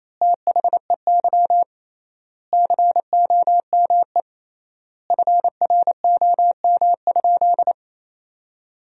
Morse code, heard as a keyed tone, about 28 wpm, 700 Hz. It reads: THEY COME FROM?
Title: 3x Word-Spacing - Sentences from Top 200 Words - 28wpm